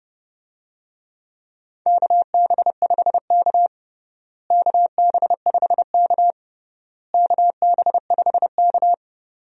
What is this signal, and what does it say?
Morse code, heard as a keyed tone, about 32 wpm, 700 Hz. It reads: KB5K KB5K KB5K
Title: USA Call Signs - 4 Characters - Repeated 3x - 30wpm